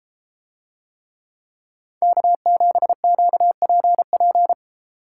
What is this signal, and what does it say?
Morse code, heard as a keyed tone, about 30 wpm, 700 Hz. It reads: K7QPP